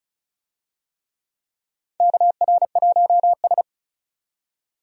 Morse code, heard as a keyed tone, about 34 wpm, 700 Hz. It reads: KR1S
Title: USA Call Signs - 4 Characters - 35wpm